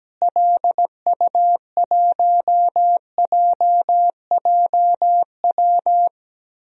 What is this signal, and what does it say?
Morse code, heard as a keyed tone, about 17 wpm, 700 Hz. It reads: LU1JJW
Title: DXCC-100 Call Signs - 6 Characters - Repeated 3x - 17wpm